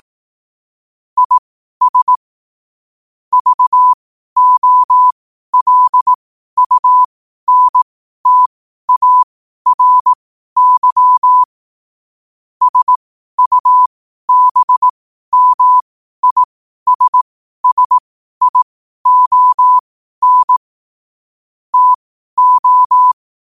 Morse code, heard as a keyed tone, about 18 wpm, spaced as for 12 wpm, 1000 Hz. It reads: IS VOLUNTARY SUBMISSION TO
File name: qotdfiles / 12wpm